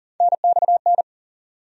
Morse code, heard as a keyed tone, about 40 wpm, 700 Hz. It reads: NXN
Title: USA Call Sign Postfixes - 3 Letters 40wpm